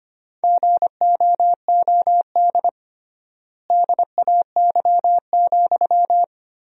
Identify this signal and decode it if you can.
Morse code, heard as a keyed tone, about 25 wpm, 700 Hz. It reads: GOOD DAY,
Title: 3x Word-Spacing - Sentences from Top 200 Words - 25wpm